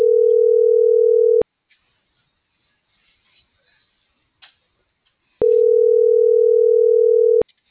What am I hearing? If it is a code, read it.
Ringback tone (North America), 440 + 480 Hz.